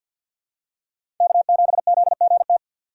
Morse code, heard as a keyed tone, about 53 wpm, 700 Hz. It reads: K7ZGT
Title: USA Call Signs - 5 Characters - 3x Character Spacing - 50wpm